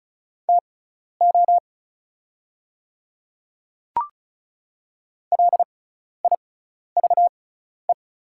Morse code, heard as a keyed tone, about 34 wpm, spaced as for 12 wpm, 690 Hz. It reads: TO LIVE
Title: 6x Character Spacing - Top 500 Words - 35wpm